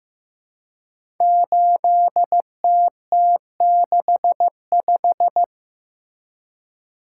Morse code, heard as a keyed tone, about 15 wpm, 700 Hz. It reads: 8TT65
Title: Code Groups - 5 Letter:Numbers - 15wpm